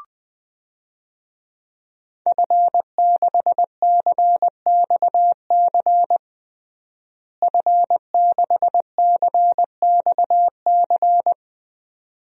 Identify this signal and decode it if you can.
Morse code, heard as a keyed tone, about 20 wpm, 700 Hz. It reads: F6CXC F6CXC
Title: DXCC-100 Call Signs - 5 Characters - Repeated 2x - 20wpm